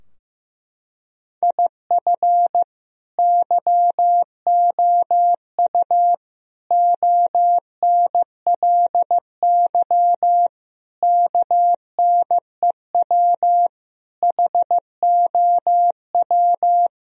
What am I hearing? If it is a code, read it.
Morse code, heard as a keyed tone, about 15 wpm, 700 Hz. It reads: IF YOU ONLY KNEW HOW
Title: Sentences from Top 1000 Words - Review Part 4 - 15wpm